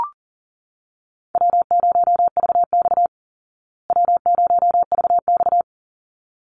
Morse code, heard as a keyed tone, about 40 wpm, 700 Hz. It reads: W0VX W0VX
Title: Easy Call Signs Repeated 2x - 40wpm